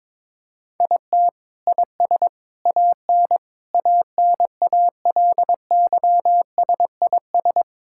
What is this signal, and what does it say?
Morse code, heard as a keyed tone, about 21 wpm, 700 Hz. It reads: IT IS AN ANALYSIS
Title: Sentences from Top 1000 Words - Review Part 5 - 22wpm